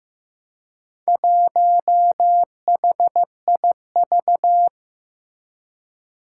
Morse code, heard as a keyed tone, about 15 wpm, 700 Hz. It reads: 1HIV
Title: Four Challenging Letter-Number Combinations 15wpm